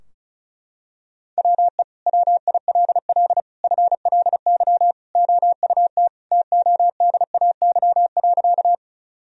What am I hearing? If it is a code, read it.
Morse code, heard as a keyed tone, about 36 wpm, 700 Hz. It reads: WE WILL FLY OUT TODAY.
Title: Sentences from Top 1000 Words - Review Part 2 - 35wpm